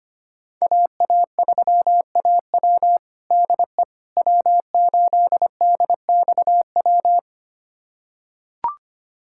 Morse code, heard as a keyed tone, about 25 wpm, 700 Hz. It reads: AA3AW DE W8DXW
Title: USA Callsign DE Callsign - 5 Characters - 25wpm